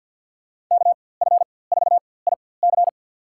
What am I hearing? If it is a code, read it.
Morse code, heard as a keyed tone, about 50 wpm, spaced as for 23 wpm, 700 Hz. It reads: KF4IC
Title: USA Call Signs - 4x Character Spacing - 50wpm